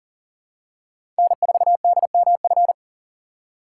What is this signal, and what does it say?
Morse code, heard as a keyed tone, about 40 wpm, 700 Hz. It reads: N4DMF
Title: USA Call Signs - 5 Characters - 2x Character Spacing - 40wpm